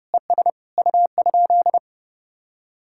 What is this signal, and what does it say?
Morse code, heard as a keyed tone, about 32 wpm, 700 Hz. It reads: ES U?